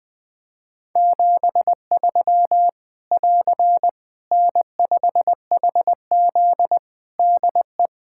Morse code, heard as a keyed tone, about 20 wpm, 700 Hz. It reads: 73 <AR> N5HZ DE
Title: Practice QSO - Exchange 4 - Ending - 20wpm